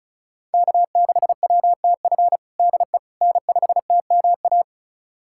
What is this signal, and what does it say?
Morse code, heard as a keyed tone, about 34 wpm, 700 Hz. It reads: K6WTF DE N5TMA
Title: USA Callsign DE Callsign - 5 Characters - 35wpm